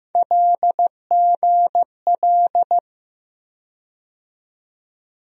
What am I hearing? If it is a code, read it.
Morse code, heard as a keyed tone, about 15 wpm, 700 Hz. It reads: LGL
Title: USA Call Sign Postfixes - 3 Letters 15wpm